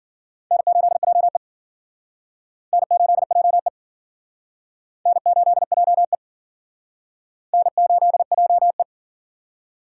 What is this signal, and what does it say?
Morse code, heard as a keyed tone, about 50 wpm, 700 Hz. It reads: N8JE N8JE N8JE N8JE